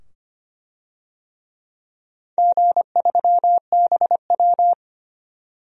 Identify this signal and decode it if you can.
Morse code, heard as a keyed tone, about 25 wpm, 700 Hz. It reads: G3BW
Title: DXCC-100 Call Signs - 4 Characters - 25wpm